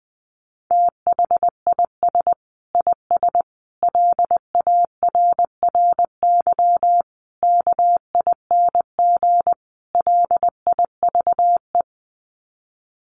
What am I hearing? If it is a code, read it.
Morse code, heard as a keyed tone, about 20 wpm, 700 Hz. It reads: THIS IS LARRY KING LIVE
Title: Sets of 5 Words Encore2 20wpm